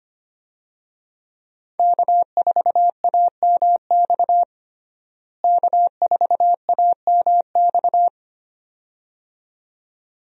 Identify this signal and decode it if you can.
Morse code, heard as a keyed tone, about 25 wpm, 700 Hz. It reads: K4AMX K4AMX
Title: USA Call Signs - 5 Characters - Repeated 2x - 25wpm